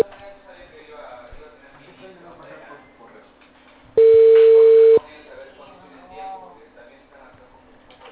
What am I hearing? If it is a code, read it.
Ringback tone (China, 450 Hz), with speech.